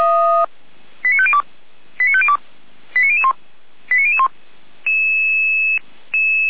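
Five-tone selective call (ZVEI), heard as quick runs of five tones, tones of about 70 ms. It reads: 89502 89502 89001 89001
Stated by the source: Five-tone ZVEI selective call received on an SDR, from the feuerwehr-orlamuende.org fire brigade audio archive.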